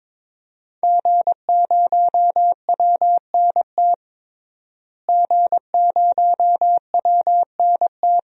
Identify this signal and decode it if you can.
Morse code, heard as a keyed tone, about 22 wpm, 700 Hz. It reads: G0WNT G0WNT
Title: DXCC-100 Call Signs - 5 Characters - Repeated 2x - 22wpm